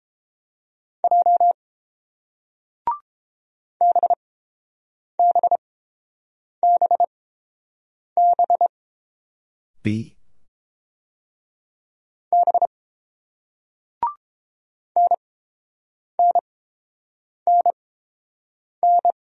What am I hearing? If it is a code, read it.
Morse code, heard as a keyed tone, about 30 wpm, spaced as for 7 wpm, 700 Hz. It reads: J BBBB B NNNN